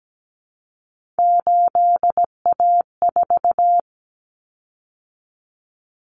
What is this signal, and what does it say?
Morse code, heard as a keyed tone, about 17 wpm, 700 Hz. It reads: 8A4
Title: Common Cut Numbers - 3-Digit 17wpm